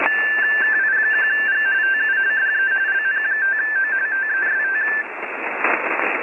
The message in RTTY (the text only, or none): KI1G TU DE LN5O TEST